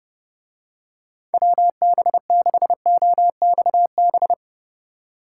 Morse code, heard as a keyed tone, about 28 wpm, 700 Hz. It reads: WB6OXB